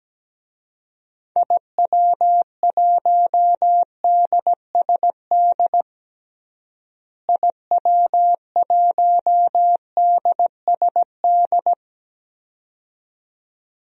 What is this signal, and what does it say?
Morse code, heard as a keyed tone, about 17 wpm, 700 Hz. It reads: IW1DSD IW1DSD
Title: DXCC-100 Call Signs - 6 Characters - Repeated 2x - 17wpm